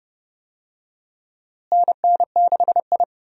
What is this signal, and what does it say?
Morse code, heard as a keyed tone, about 30 wpm, 700 Hz. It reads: NN6I
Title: USA Call Signs - 4 Characters - Repeated 3x - 30wpm